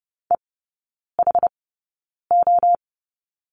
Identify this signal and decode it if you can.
Morse code, heard as a keyed tone, about 30 wpm, spaced as for 9 wpm, 700 Hz. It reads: EHO
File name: 081 Lesson 14e - Number 2 - Three Characters 30wpm